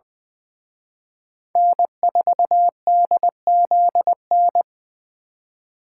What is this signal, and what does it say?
Morse code, heard as a keyed tone, about 23 wpm, 700 Hz. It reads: N4DZN
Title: Speed Racing - USA 1x3 Call Signs 20wpm